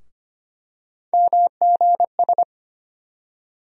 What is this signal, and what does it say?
Morse code, heard as a keyed tone, about 25 wpm, 700 Hz. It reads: MGS
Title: USA Call Sign Postfixes - 3 Letters 25wpm